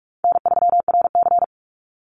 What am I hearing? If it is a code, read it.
Morse code, heard as a keyed tone, about 44 wpm, 700 Hz. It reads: N3RC
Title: Easy Call Signs - 45wpm